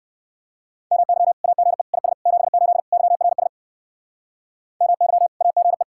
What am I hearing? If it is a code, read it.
Morse code, heard as a keyed tone, about 55 wpm, 700 Hz. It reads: MY AGE IS 78 YRS MY AGE
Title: Speed-Racing Common QSO Phrases - 45wpm